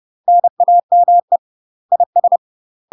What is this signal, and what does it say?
Morse code, heard as a keyed tone, about 30 wpm, 700 Hz. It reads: NAME IS
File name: UK MASTER QSOs TRAINING / 30WPM / UK MASTER QSOs 0051